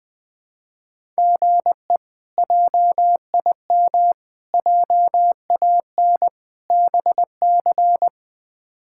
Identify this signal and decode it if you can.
Morse code, heard as a keyed tone, about 20 wpm, 700 Hz. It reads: GE JIM JAN BC
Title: K1USN SST QSO Practice - 20wpm